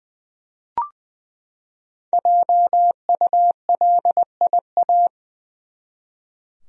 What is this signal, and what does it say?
Morse code, heard as a keyed tone, about 20 wpm, 700 Hz. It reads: JULIA
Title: USA Common Names - 20wpm